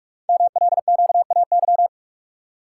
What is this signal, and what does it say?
Morse code, heard as a keyed tone, about 44 wpm, 690 Hz. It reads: MLQAY
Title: Code Groups - 5 Letters - 45wpm